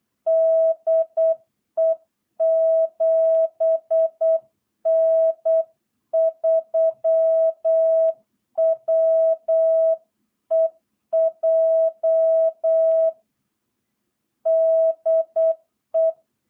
Morse code, heard as a keyed tone, about 8 wpm, 660 Hz. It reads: DE 7N3WEJ DE
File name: jh8cbh / 2022cw